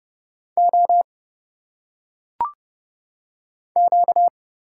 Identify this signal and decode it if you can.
Morse code, heard as a keyed tone, about 30 wpm, 700 Hz. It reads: O Q
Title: Single Letters - Phonetic Pronunciation 30wpm